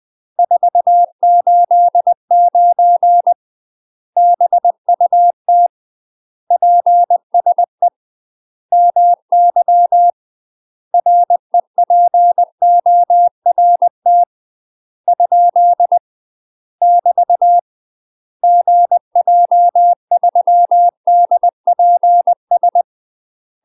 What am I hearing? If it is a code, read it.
Morse code, heard as a keyed tone, about 20 wpm, 700 Hz. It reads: 489 BUT PSE MY REPORT ? <BT> GJ3DPS